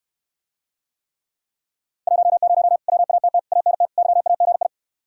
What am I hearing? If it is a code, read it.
Morse code, heard as a keyed tone, about 67 wpm, 700 Hz. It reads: 10 WATT ANT YAGI